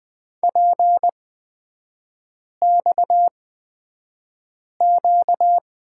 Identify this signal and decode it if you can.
Morse code, heard as a keyed tone, about 20 wpm, spaced as for 5 wpm, 700 Hz. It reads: PXQ
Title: 255 Lesson 41e - Period - Three Characters 20wpm